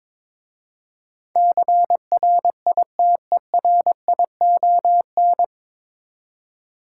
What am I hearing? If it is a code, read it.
Morse code, heard as a keyed tone, about 23 wpm, 700 Hz. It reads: CRITERION
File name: 9 Letter Words - First 3 Letters Given - 22wpm